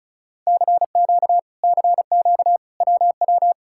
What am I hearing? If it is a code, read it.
Morse code, heard as a keyed tone, about 48 wpm, spaced as for 42 wpm, 700 Hz. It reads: CQ CQ WW
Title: Speed-Racing Common QSO Phrases - 35wpm